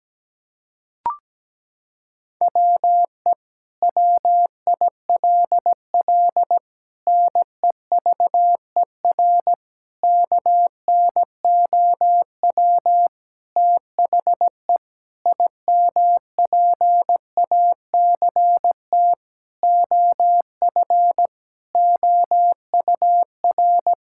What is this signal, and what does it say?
Morse code, heard as a keyed tone, about 17 wpm, 700 Hz. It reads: WE WILL NEVER KNOW THE IMPACT OF OUR
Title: Sentences from Top 1000 Words - Review Part 6 - 17wpm